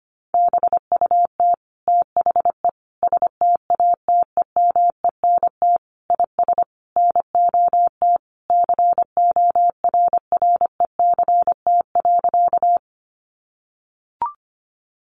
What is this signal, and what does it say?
Morse code, heard as a keyed tone, about 25 wpm, 700 Hz. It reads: BUT THE STATEMENT IS NOT CORRECT.
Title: Sentences from Top 1200 Words 25wpm